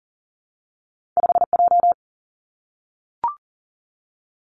Morse code, heard as a keyed tone, about 38 wpm, 700 Hz. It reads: 5 J
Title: Two Challenging Letter-Number Combinations 40wpm